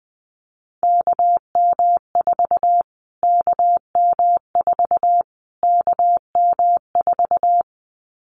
Morse code, heard as a keyed tone, about 20 wpm, 700 Hz. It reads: KM4 KM4 KM4
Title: Callsign Prefixes for Top 25 DXCC - Repeated 3x - 20wpm